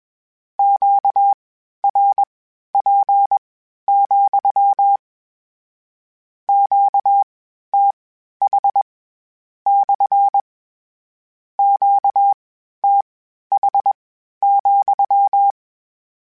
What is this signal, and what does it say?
Morse code, heard as a keyed tone, about 21 wpm, spaced as for 12 wpm, 800 Hz. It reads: QRP, QTH / QTH,
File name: Q Calls / Q Calls-mp3s / Q Calls SPSL2 @20